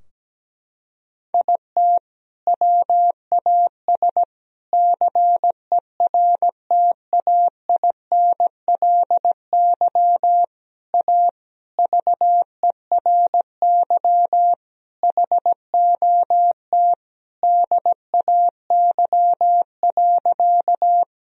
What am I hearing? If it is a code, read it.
Morse code, heard as a keyed tone, about 17 wpm, 700 Hz. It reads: IT WAS CERTAINLY A VERY HOT DAY.
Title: Sentences from Top 1000 Words - Review Part 3 - 17wpm